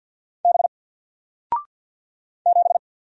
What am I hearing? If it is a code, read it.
Morse code, heard as a keyed tone, about 50 wpm, 700 Hz. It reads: B 7
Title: Single Letter-Number 50wpm